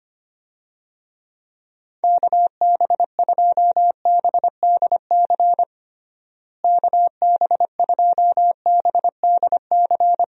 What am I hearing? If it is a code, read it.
Morse code, heard as a keyed tone, about 25 wpm, 700 Hz. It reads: KB2BDC KB2BDC
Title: USA Call Signs - 6 Characters - Repeated 2x - 25wpm